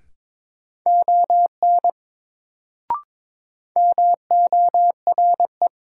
Morse code, heard as a keyed tone, about 22 wpm, 700 Hz. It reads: ON MORE
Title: Sentences from Top 100 Words - New Word Drill - 22wpm